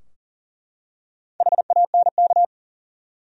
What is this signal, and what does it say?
Morse code, heard as a keyed tone, about 40 wpm, 700 Hz. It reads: HANK
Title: USA Common Names - Max 4 Letters - 40wpm